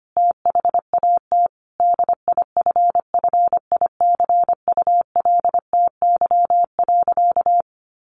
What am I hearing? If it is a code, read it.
Morse code, heard as a keyed tone, about 25 wpm, 700 Hz. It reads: THAT DIFFICULTY.
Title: Sentences from Top 1500 Words 25wpm